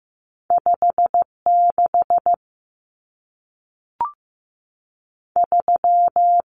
Morse code, heard as a keyed tone, about 15 wpm, spaced as for 3 wpm, 700 Hz. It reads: 5 63
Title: Numbers - 2-Digit 15wpm